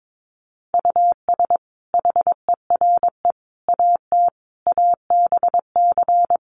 Morse code, heard as a keyed tone, about 22 wpm, 700 Hz. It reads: US HERE AT ABC
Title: Sets of 5 Words 22wpm